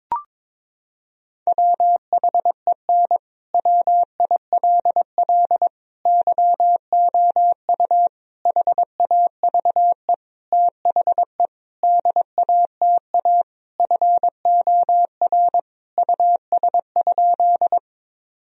Morse code, heard as a keyed tone, about 22 wpm, 700 Hz. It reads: WHEN WILL YOU HAVE THE DATA FOR US?
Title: Sentences from Top 1000 Words - Review Part 3 - 22wpm